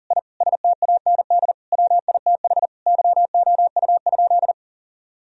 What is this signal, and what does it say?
Morse code, heard as a keyed tone, about 37 wpm, 700 Hz. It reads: I STAND WITH YOU?
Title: Sentences from Top 1000 Words - Review Part 1 - 40wpm